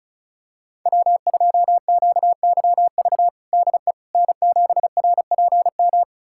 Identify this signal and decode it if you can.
Morse code, heard as a keyed tone, about 36 wpm, 700 Hz. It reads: W2QYV DE N7RPM
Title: USA Callsign DE Callsign - 5 Characters - 35wpm